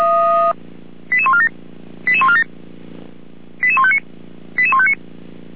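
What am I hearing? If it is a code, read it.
Five-tone selective call (ZVEI), heard as quick runs of five tones, tones of about 70 ms. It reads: 88147 88147 88159 88159
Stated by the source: ZVEI five-tone call received by SDR.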